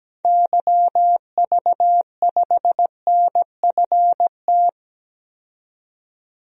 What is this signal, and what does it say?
Morse code, heard as a keyed tone, about 17 wpm, 700 Hz. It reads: YV5NFT
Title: DXCC-100 Call Signs - 6 Characters - 17wpm